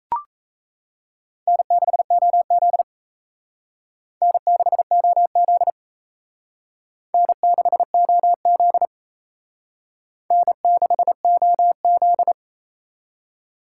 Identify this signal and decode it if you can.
Morse code, heard as a keyed tone, about 35 wpm, 700 Hz. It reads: N6OZ N6OZ N6OZ N6OZ
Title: Speed Racing - USA 1x2 Call Signs 28wpm